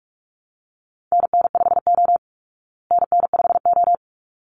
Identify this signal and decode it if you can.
Morse code, heard as a keyed tone, about 44 wpm, 700 Hz. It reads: NN5O NN5O
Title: Call Signs Repeated 3x - 45wpm